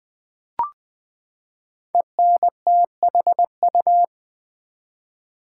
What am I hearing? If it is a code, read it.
Morse code, heard as a keyed tone, about 20 wpm, 700 Hz. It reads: ENTHU
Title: Build A Word - 5 Letters At A Time - 12-Letter Words - 20wpm